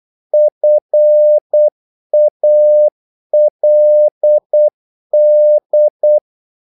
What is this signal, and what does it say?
Morse code, heard as a keyed tone, about 8 wpm, 600 Hz. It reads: FALD
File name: Korte ord DK 8wpm